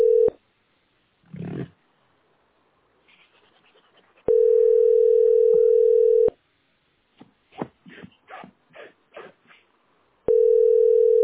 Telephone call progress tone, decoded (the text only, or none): ringback tone